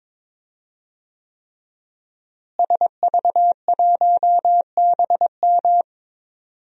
Morse code, heard as a keyed tone, about 22 wpm, 700 Hz. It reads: SV1BM
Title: DXCC-100 Call Signs - 5 Characters - Repeated 3x - 22wpm